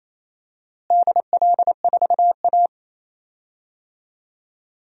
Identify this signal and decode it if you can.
Morse code, heard as a keyed tone, about 28 wpm, 700 Hz. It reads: DL4A
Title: DXCC-100 Call Signs - 4 Characters - Repeated 2x - 28wpm